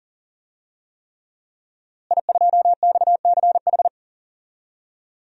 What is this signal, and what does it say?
Morse code, heard as a keyed tone, about 40 wpm, 700 Hz. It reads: I2XCH
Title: DXCC-100 Call Signs - 5 Characters - Repeated 3x - 40wpm